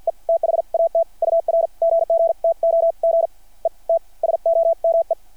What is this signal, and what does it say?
Morse code, heard as a keyed tone, about 52 wpm, spaced as for 39 wpm, 660 Hz. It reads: E THAT UR GG TO GET SOME
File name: archived / CRSnet